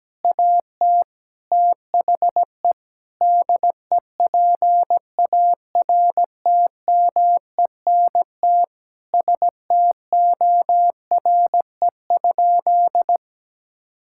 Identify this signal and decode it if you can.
Morse code, heard as a keyed tone, about 17 wpm, 700 Hz. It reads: AT THE DEPARTMENT STORE?
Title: Sentences from Top 1000 Words - Review Part 5 - 17wpm